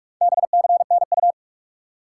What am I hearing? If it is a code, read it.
Morse code, heard as a keyed tone, about 44 wpm, 700 Hz. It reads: BCNU
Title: Top 90 QSO Elements 45wpm